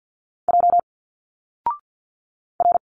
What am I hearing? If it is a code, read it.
Morse code, heard as a keyed tone, about 50 wpm, 700 Hz. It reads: J R
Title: Single Letters 50wpm